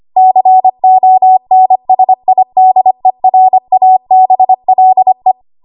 Morse code, heard as a keyed tone, about 25 wpm, 750 Hz. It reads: CONSIDERABLE
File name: Archive / 25 WPM / 25WPM